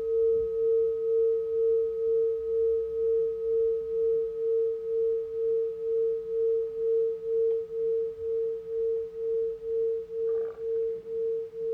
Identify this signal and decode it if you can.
Dial tone (China, 450 Hz).